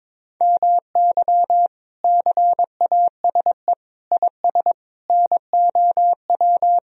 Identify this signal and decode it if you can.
Morse code, heard as a keyed tone, about 21 wpm, 700 Hz. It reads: MY CASE IS NOW
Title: Sentences from Top 1000 Words - Review Part 2 - 22wpm